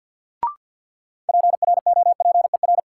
Morse code, heard as a keyed tone, about 49 wpm, 700 Hz. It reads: PROPER
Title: Sentences from Top 800 Words - New Word Drill - 50wpm